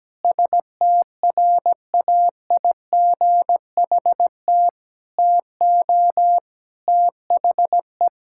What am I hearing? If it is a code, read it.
Morse code, heard as a keyed tone, about 17 wpm, 700 Hz. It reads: STRAIGHT TO THE
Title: Sentences from Top 1000 Words - Review Part 6 - 17wpm